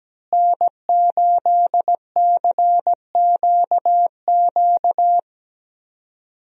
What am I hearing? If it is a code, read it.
Morse code, heard as a keyed tone, about 17 wpm, 700 Hz. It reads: N8CQQ
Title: USA Call Signs - 5 Characters - Repeated 3x - 17wpm